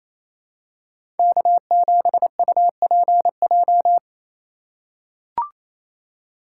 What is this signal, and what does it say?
Morse code, heard as a keyed tone, about 27 wpm, 700 Hz. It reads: K7UPJ
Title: K1USN SST QSO Practice - 28wpm